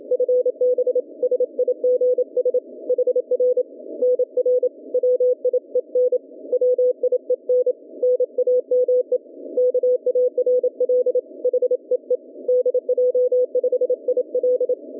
Morse code, heard as a keyed tone, about 27 wpm, 440 Hz. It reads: FB SIGS HR NR WIEN WIEN NAME KARL HEE DJ5IL